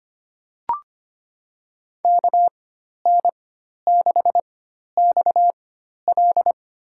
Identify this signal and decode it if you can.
Morse code, heard as a keyed tone, about 25 wpm, spaced as for 12 wpm, 700 Hz. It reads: KN6XL
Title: USA Call Signs - 5 Characters - 4x Character Spacing - 25wpm